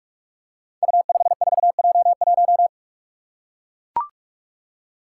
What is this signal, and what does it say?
Morse code, heard as a keyed tone, about 44 wpm, 700 Hz. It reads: U54J1
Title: Five Challenging Letter-Number Combinations 45wpm